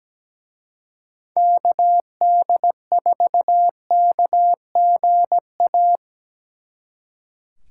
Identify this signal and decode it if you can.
Morse code, heard as a keyed tone, about 19 wpm, 700 Hz. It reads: KD4KGA
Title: Speed Racing - USA 2x3 Call Signs 17wpm